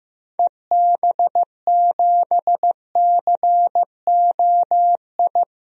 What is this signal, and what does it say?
Morse code, heard as a keyed tone, about 15 wpm, 690 Hz. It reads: EB7COI
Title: DXCC-100 Call Signs - 6 Characters - 15wpm